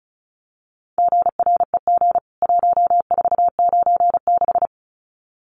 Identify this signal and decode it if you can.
Morse code, heard as a keyed tone, about 34 wpm, 700 Hz. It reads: GREG 1496
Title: CWT Exchange 35wpm